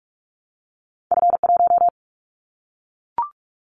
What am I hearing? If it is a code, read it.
Morse code, heard as a keyed tone, about 44 wpm, 700 Hz. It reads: F1
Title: Callsign Prefixes for Top 25 DXCC - Repeated 3x - 45wpm